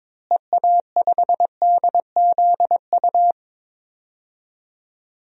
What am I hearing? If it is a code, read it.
Morse code, heard as a keyed tone, about 22 wpm, 700 Hz. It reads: EA5DZU
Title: DXCC-100 Call Signs - 6 Characters - Repeated 3x - 22wpm